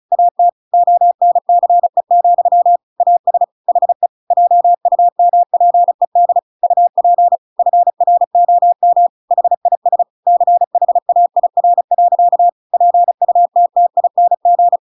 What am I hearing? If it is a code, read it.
Morse code, heard as a keyed tone, about 34 wpm, 700 Hz. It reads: AT ONCE, AS HE JUMPED UP FROM HIS CHAIR. PUTTING
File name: Pinocchio / Pinocchio ch8 35 35 ik1hge